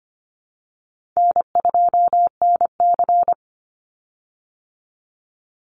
Morse code, heard as a keyed tone, about 25 wpm, 700 Hz. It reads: N2NC
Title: Easy Call Signs - 25wpm